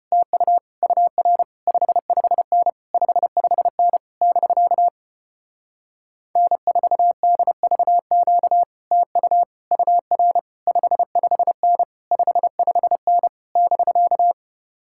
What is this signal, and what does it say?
Morse code, heard as a keyed tone, about 33 wpm, 700 Hz. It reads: TU UR 55N 55N <BK> N4DVQ TU UR 55N 55N <BK>